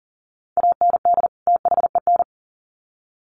Morse code, heard as a keyed tone, about 40 wpm, 700 Hz. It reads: AND THEN
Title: Sets of 2 Words 40wpm